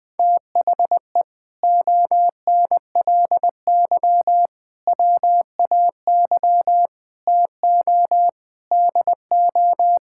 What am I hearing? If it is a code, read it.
Morse code, heard as a keyed tone, about 19 wpm, 700 Hz. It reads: THE ONLY WAY TO DO
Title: Sentences with 8-Word Phrases - 20wpm